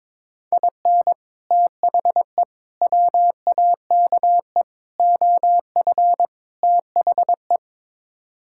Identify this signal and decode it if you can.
Morse code, heard as a keyed tone, about 22 wpm, 700 Hz. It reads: IN THE WAKE OF THE
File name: practice / 22wpm / Build A Phrase - 5-Word Phrases - 22wpm